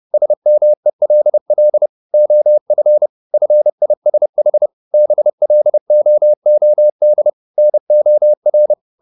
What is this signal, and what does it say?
Morse code, heard as a keyed tone, about 30 wpm, 600 Hz. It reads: SMELL OF FISH BLOOD NOR